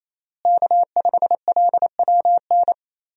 Morse code, heard as a keyed tone, about 28 wpm, 700 Hz. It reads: K5LWN
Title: USA Call Signs - 5 Characters - Repeated 3x - 28wpm